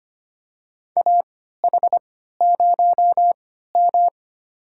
Morse code, heard as a keyed tone, about 25 wpm, spaced as for 14 wpm, 700 Hz. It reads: AH0M
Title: USA Call Signs - 4 Characters - 3x Character Spacing - 25wpm